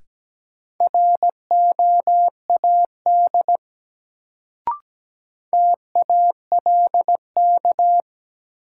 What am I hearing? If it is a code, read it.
Morse code, heard as a keyed tone, about 17 wpm, 700 Hz. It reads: ROAD TALK
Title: Sentences from Top 400 Words - New Word Drill - 17wpm